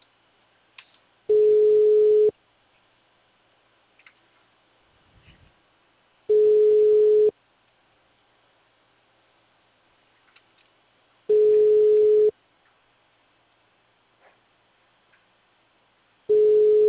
Ringback tone (Europe, 425 Hz).